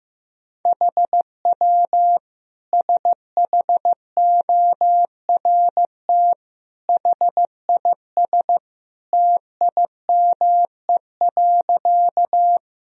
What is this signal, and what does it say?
Morse code, heard as a keyed tone, about 15 wpm, 700 Hz. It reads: HW SHORT HIS TIME.
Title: Sentences from Top 400 Words and Abbreviations 15wpm